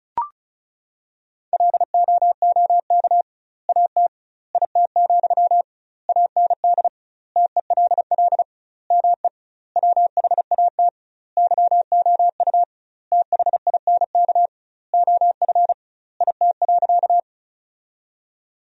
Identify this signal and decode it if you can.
Morse code, heard as a keyed tone, about 36 wpm, 700 Hz. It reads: LOOK AT IT, AND TELL ME WHAT YOU THINK OF IT.
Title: 2x Word-Spacing - Sentences from Top 200 Words - 35wpm